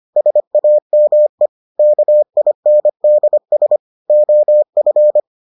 Morse code, heard as a keyed tone, about 25 wpm, 600 Hz. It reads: SAME KINDS OF